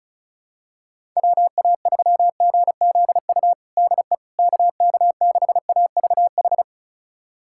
Morse code, heard as a keyed tone, about 36 wpm, 700 Hz. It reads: WA3GZU DE KK6AVH
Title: USA Callsign DE Callsign - 6 Characters - 35wpm